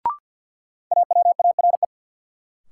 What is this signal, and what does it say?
Morse code, heard as a keyed tone, about 51 wpm, 700 Hz. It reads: AWARE
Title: Sentences from Top 1200 Words - New Word Drill - 50wpm